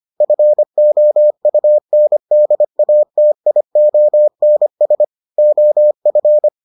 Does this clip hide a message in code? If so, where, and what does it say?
Yes: Morse code from 0.2 to 6.5 s, heard as a keyed tone, about 25 wpm, 600 Hz. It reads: FOUNDATIONS OF